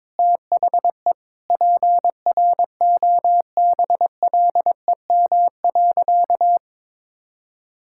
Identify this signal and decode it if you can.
Morse code, heard as a keyed tone, about 22 wpm, 700 Hz. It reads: THE PROBLEM.
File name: Sentences with 7-Word Phrases - 22wpm